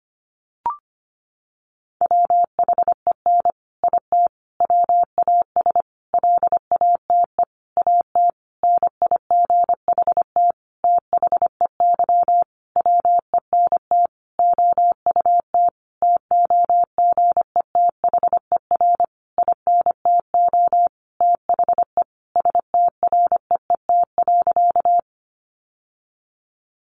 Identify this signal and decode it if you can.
Morse code, heard as a keyed tone, about 25 wpm, 700 Hz. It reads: WHEN IT WAS LATE AT NIGHT THEY WENT OUT TOGETHER INTO THE STREET.
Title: Sentences from Top 600 Words 25wpm